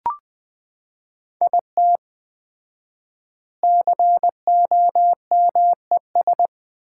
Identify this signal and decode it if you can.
Morse code, heard as a keyed tone, about 20 wpm, 700 Hz. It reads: IT COMES
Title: 4x Word-Spacing - Sets of 3 Words - 20wpm